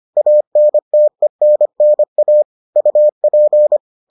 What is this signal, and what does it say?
Morse code, heard as a keyed tone, about 25 wpm, 600 Hz. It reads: ANTENNA UP